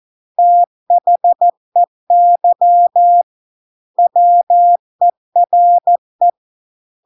Morse code, heard as a keyed tone, about 14 wpm, 700 Hz. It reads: THEY WERE